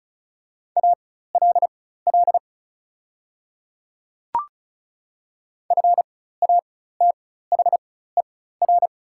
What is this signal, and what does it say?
Morse code, heard as a keyed tone, about 36 wpm, spaced as for 16 wpm, 700 Hz. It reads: ALL FATHER
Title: 4x Character Spacing - Top 300 Words - 35wpm